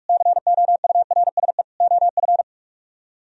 Morse code, heard as a keyed tone, about 46 wpm, 700 Hz. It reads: COURSE OF